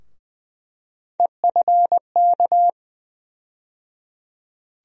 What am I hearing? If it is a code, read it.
Morse code, heard as a keyed tone, about 20 wpm, 700 Hz. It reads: EFK
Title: USA Call Sign Postfixes - 3 Letters 20wpm